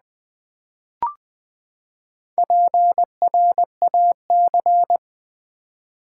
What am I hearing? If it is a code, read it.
Morse code, heard as a keyed tone, about 20 wpm, 700 Hz. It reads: PRAC
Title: Build A Word - 4 Letters At A Time - 12-Letter Words - 20wpm